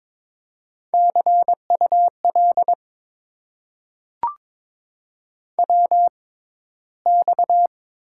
Morse code, heard as a keyed tone, about 22 wpm, spaced as for 13 wpm, 700 Hz. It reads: CUL WX
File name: Top 90 QSO Elements - 6x Character Spacing - 22wpm